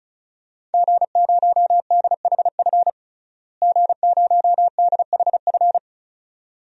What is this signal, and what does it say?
Morse code, heard as a keyed tone, about 36 wpm, 700 Hz. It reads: G0DHF G0DHF
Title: DXCC-100 Call Signs - 5 Characters - Repeated 2x - 35wpm